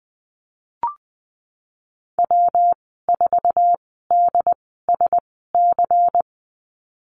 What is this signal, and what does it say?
Morse code, heard as a keyed tone, about 20 wpm, spaced as for 14 wpm, 700 Hz. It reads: W4DSC
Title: 161 Lesson 26g - Letter g - Call Signs 20wpm